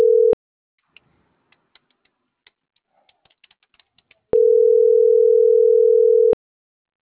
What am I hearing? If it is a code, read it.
Ringback tone (North America), 440 + 480 Hz.